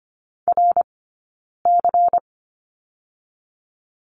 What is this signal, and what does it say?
Morse code, heard as a keyed tone, about 25 wpm, 700 Hz. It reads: R C
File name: Lesson 17d - Letter u - Two Characters 25wpm